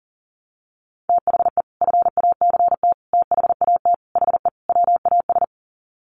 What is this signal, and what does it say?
Morse code, heard as a keyed tone, about 40 wpm, 700 Hz. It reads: THE FACT THAT HE WAS
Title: Sets of 5 Words 40wpm